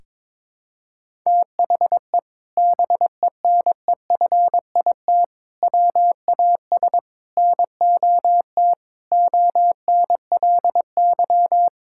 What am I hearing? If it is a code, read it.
Morse code, heard as a keyed tone, about 22 wpm, 700 Hz. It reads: THE BENEFIT WAS NOT ONLY
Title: Sentences from Top 1000 Words - Review Part 6 - 22wpm